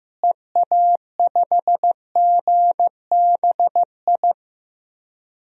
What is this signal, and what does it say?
Morse code, heard as a keyed tone, about 15 wpm, 700 Hz. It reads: EA5GBI
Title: DXCC-100 Call Signs - 6 Characters - Repeated 3x - 15wpm